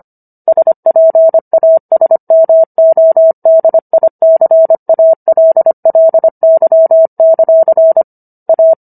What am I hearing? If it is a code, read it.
Morse code, heard as a keyed tone, about 25 wpm, 650 Hz. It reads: SPASMODICALLY; A